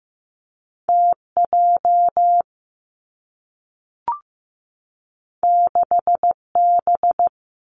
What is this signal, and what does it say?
Morse code, heard as a keyed tone, about 15 wpm, 700 Hz. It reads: T J 6 B
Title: Two Challenging Letter-Number Combinations 15wpm